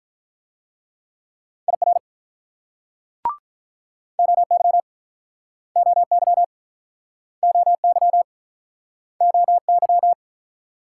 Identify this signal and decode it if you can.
Morse code, heard as a keyed tone, about 45 wpm, 700 Hz. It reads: IR OY OY OY OY